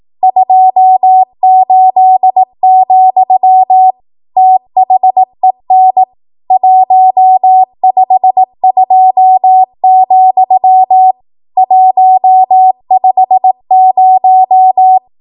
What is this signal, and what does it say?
Morse code, heard as a keyed tone, about 18 wpm, 750 Hz. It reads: 28, THEN 152, 150
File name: Archive / 18 WPM / 18WPM